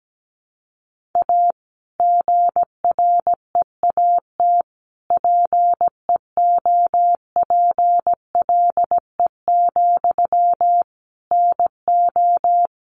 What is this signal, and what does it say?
Morse code, heard as a keyed tone, about 17 wpm, 700 Hz. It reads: A GREAT PEOPLE, NO
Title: Sentences from Top 1500 Words 17wpm